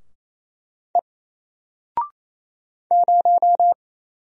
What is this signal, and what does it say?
Morse code, heard as a keyed tone, about 27 wpm, 700 Hz. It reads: E 0
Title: Single Letter-Number Alt 28wpm